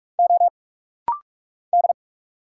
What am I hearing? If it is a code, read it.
Morse code, heard as a keyed tone, about 46 wpm, 700 Hz. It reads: O D
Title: Single Letters - Rapid-Fire 45wpm